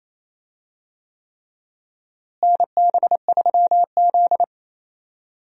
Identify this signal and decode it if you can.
Morse code, heard as a keyed tone, about 28 wpm, 700 Hz. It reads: NB3Z
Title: USA Call Signs - 4 Characters - Repeated 3x - 28wpm